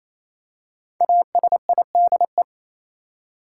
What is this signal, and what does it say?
Morse code, heard as a keyed tone, about 28 wpm, 700 Hz. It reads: ASIDE
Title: Five Letter Words - 28wpm